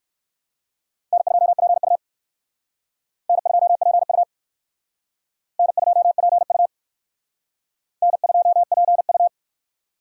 Morse code, heard as a keyed tone, about 56 wpm, 700 Hz. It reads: N2PU N2PU N2PU N2PU
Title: Speed Racing - USA Call Signs 45wpm